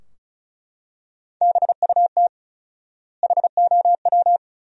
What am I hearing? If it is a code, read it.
Morse code, heard as a keyed tone, about 34 wpm, 700 Hz. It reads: BUT HOW
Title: 4x Word-Spacing - Sentences from Top 400 Words - 35wpm